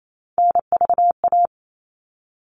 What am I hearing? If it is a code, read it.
Morse code, heard as a keyed tone, about 28 wpm, 700 Hz. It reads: NVA
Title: All Cut Numbers - 3-Digit 28wpm